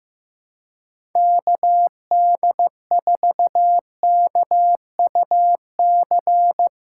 Morse code, heard as a keyed tone, about 15 wpm, 700 Hz. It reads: KD4KUC
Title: USA Callsign DE Callsign 15wpm